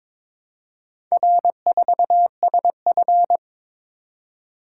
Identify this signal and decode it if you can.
Morse code, heard as a keyed tone, about 23 wpm, 700 Hz. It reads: R4SF
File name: practice / 22wpm / Four Letter-Number Combinations - 22wpm